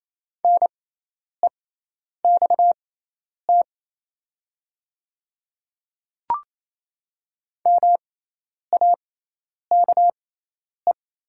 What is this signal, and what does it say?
Morse code, heard as a keyed tone, about 28 wpm, spaced as for 10 wpm, 700 Hz. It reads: NEXT MAKE
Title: 6x Character Spacing - Top 300 Words - 28wpm